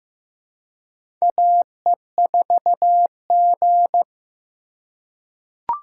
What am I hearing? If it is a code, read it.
Morse code, heard as a keyed tone, about 15 wpm, 700 Hz. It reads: AE4G
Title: USA Call Signs - 4x Character Spacing - 15wpm